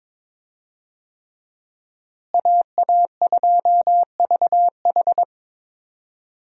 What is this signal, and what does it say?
Morse code, heard as a keyed tone, about 22 wpm, 700 Hz. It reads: AA2VH
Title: USA Call Signs - 5 Characters - Repeated 3x - 22wpm